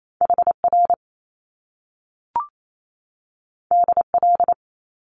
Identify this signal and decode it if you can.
Morse code, heard as a keyed tone, about 28 wpm, 700 Hz. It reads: HR DL
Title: Top Two Letter Combinations in Words 28wpm